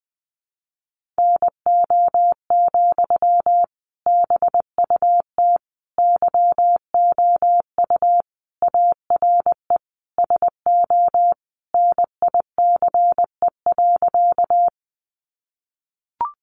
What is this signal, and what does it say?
Morse code, heard as a keyed tone, about 20 wpm, 700 Hz. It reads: NO, BUT YOU ARE SO NICE.
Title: Sentences from Top 1100 Words 20wpm